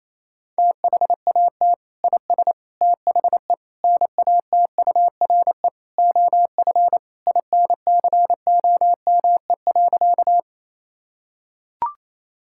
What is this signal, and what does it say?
Morse code, heard as a keyed tone, about 28 wpm, 700 Hz. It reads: THAT IS THE NATURE OF INCOME.
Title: Sentences from Top 1000 Words - Review Part 5 - 28wpm